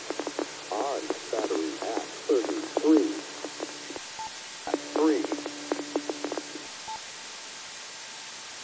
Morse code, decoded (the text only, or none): none